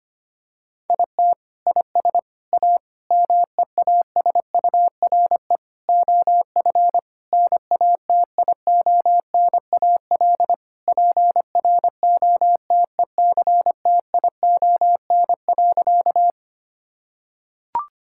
Morse code, heard as a keyed tone, about 25 wpm, 700 Hz. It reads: IT IS A MEASURE OF NATIONAL PROTECTION.
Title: Sentences from Top 1600 Words 25wpm